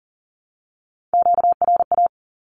Morse code, heard as a keyed tone, about 40 wpm, 700 Hz. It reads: QRA
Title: Q-Codes 40wpm